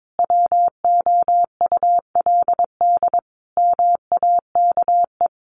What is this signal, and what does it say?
Morse code, heard as a keyed tone, about 22 wpm, 700 Hz. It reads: WOULD MAKE
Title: Sentences from Top 1300 Words 22wpm